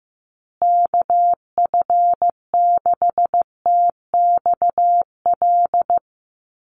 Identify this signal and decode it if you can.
Morse code, heard as a keyed tone, about 15 wpm, 700 Hz. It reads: KF6TXL
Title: USA Call Signs Repeated 3x 15wpm